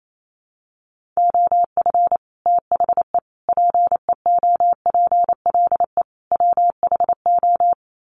Morse code, heard as a keyed tone, about 28 wpm, 700 Hz. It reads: OF THE PEOPLE WHO
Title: Sets of 4 Words 28wpm